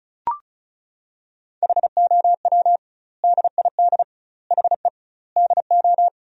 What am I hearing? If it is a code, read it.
Morse code, heard as a keyed tone, about 36 wpm, 700 Hz. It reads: HOW DID HE DO
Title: 2x Word-Spacing - Sentences from Top 100 Words - 35wpm